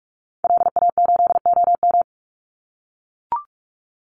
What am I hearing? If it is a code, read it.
Morse code, heard as a keyed tone, about 44 wpm, 700 Hz. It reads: LA8OM
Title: Call Signs - 45wpm